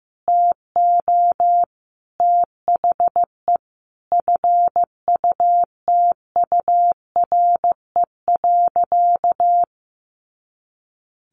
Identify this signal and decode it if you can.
Morse code, heard as a keyed tone, about 15 wpm, 700 Hz. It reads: TO THE FUTURE.
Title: A Sound Of Thunder 15wpm